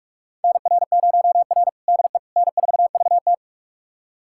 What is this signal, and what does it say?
Morse code, heard as a keyed tone, about 44 wpm, 690 Hz. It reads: NR0R DE N4VT
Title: USA Callsign DE Callsign - 4 Characters - 45wpm